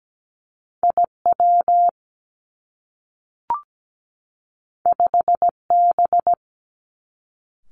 Morse code, heard as a keyed tone, about 17 wpm, 700 Hz. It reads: IW 5B
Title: Two Challenging Letter-Number Combinations 17wpm